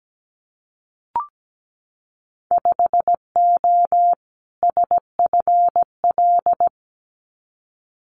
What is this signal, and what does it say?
Morse code, heard as a keyed tone, about 17 wpm, 700 Hz. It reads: 5O SFL
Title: Winter Field Day Exchanges 17wpm